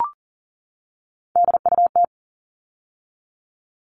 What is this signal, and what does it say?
Morse code, heard as a keyed tone, about 40 wpm, 700 Hz. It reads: DUT
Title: All Cut Numbers - 3-Digit 40wpm